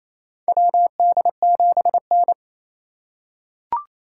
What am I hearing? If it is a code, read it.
Morse code, heard as a keyed tone, about 27 wpm, 700 Hz. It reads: WD7N
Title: USA Call Signs - 4 Characters - 4x Character Spacing - 28wpm